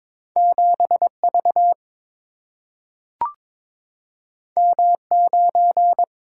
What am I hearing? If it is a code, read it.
Morse code, heard as a keyed tone, about 22 wpm, 700 Hz. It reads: 7 V M 9
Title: Single-Letter Single-Number 22wpm